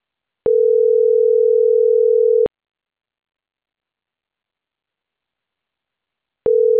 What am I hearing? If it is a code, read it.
Ringback tone (North America), 440 + 480 Hz.